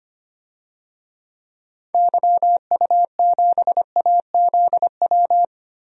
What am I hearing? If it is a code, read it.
Morse code, heard as a keyed tone, about 25 wpm, 700 Hz. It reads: YU7AZW
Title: DXCC-100 Call Signs - 6 Characters - Repeated 3x - 25wpm